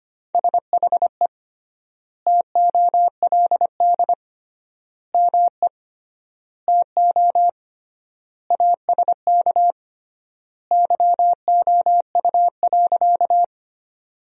Morse code, heard as a keyed tone, about 25 wpm, 700 Hz. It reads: SHE TOLD ME TO ASK YOU.
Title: 3x Word-Spacing - Sentences from Top 400 Words - 25wpm